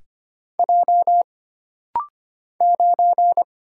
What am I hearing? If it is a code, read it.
Morse code, heard as a keyed tone, about 25 wpm, 700 Hz. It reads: J 9
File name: Single Character (Letter:Number:Punctuation:Prosign) - Rapid-Fire - 25wpm